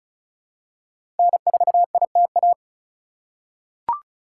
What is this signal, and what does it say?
Morse code, heard as a keyed tone, about 36 wpm, 700 Hz. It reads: N4ITA
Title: USA Call Signs - 5 Characters - 2x Character Spacing - 35wpm